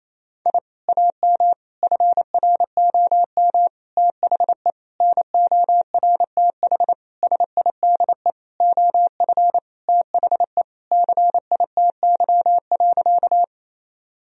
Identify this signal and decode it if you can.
Morse code, heard as a keyed tone, about 28 wpm, 700 Hz. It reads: I AM FROM THE NORTH SIDE OF THE CITY.
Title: Sentences from Top 1000 Words - Review Part 1 - 28wpm